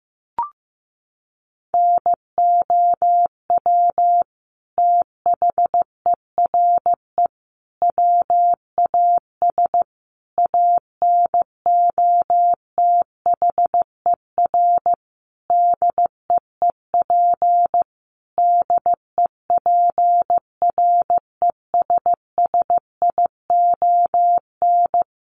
Morse code, heard as a keyed tone, about 15 wpm, 700 Hz. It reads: NOW THERE WAS ANOTHER DEEP DEPRESSION
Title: Sentences from Top 1300 Words 15wpm